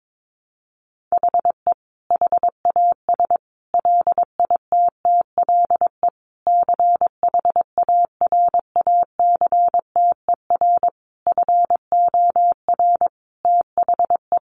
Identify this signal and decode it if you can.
Morse code, heard as a keyed tone, about 22 wpm, 700 Hz. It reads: HE HAS LITTLE CHARACTER FOR THE
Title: Sentences from Top 1200 Words 22wpm